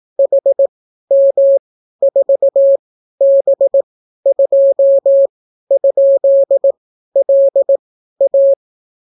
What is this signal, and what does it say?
Morse code, heard as a keyed tone, about 18 wpm, spaced as for 12 wpm, 550 Hz. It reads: HM4B2?LA